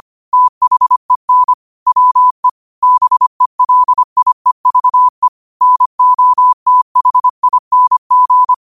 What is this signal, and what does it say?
Morse code, heard as a keyed tone, about 25 wpm, 1000 Hz. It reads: THEN WE BELIEVE NOTHING
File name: qotdfiles / 25wpm